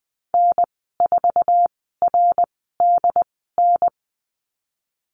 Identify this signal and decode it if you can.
Morse code, heard as a keyed tone, about 19 wpm, spaced as for 14 wpm, 700 Hz. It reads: N4RDN
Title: 076 Lesson 13g - Letter L - Call Signs 20wpm